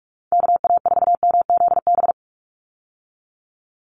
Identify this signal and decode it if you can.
Morse code, heard as a keyed tone, about 44 wpm, 700 Hz. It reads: KA4MZB
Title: USA Call Signs 45wpm